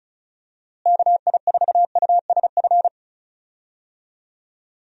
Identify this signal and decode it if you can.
Morse code, heard as a keyed tone, about 34 wpm, 700 Hz. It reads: KI4USF
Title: USA Call Signs - 6 Characters - Repeated 2x - 35wpm